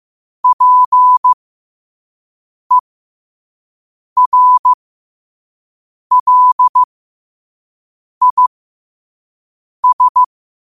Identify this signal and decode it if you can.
Morse code, heard as a keyed tone, about 15 wpm, spaced as for 5 wpm, 1000 Hz. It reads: PERLIS
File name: qotdfiles / 5wpm